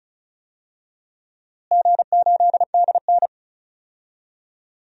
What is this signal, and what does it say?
Morse code, heard as a keyed tone, about 36 wpm, 700 Hz. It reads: G8DN
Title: DXCC-100 Call Signs - 4 Characters - Repeated 3x - 35wpm